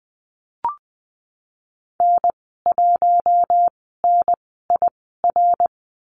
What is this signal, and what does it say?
Morse code, heard as a keyed tone, about 20 wpm, spaced as for 15 wpm, 700 Hz. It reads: N1NIR
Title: 055 Lesson 10g - Letter r - Call Signs 20wpm